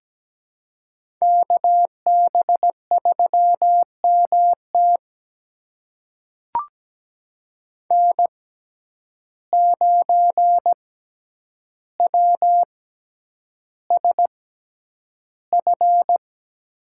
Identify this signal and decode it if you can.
Morse code, heard as a keyed tone, about 17 wpm, spaced as for 9 wpm, 700 Hz. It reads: KB3MT N9WSF
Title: USA Call Signs - 5 Characters - 6x Character Spacing - 17wpm